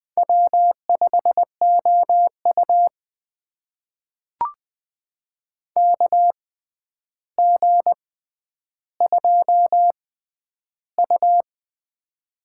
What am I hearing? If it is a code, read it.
Morse code, heard as a keyed tone, about 21 wpm, spaced as for 10 wpm, 700 Hz. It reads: W5OU KG2U
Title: USA Call Signs - 4 Characters - 6x Character Spacing - 20wpm